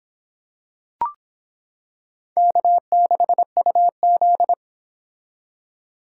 Morse code, heard as a keyed tone, about 26 wpm, 700 Hz. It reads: K6UZ